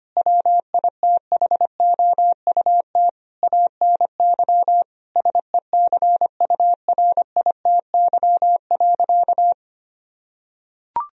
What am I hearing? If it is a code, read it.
Morse code, heard as a keyed tone, about 25 wpm, 700 Hz. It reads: WITHOUT ANY SECURITY.
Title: Sentences from Top 1000 Words - Review Part 4 - 25wpm